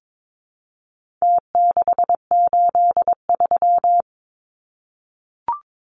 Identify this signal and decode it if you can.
Morse code, heard as a keyed tone, about 22 wpm, 700 Hz. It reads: T683
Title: Numbers - Contest Style - 4-Digit 22wpm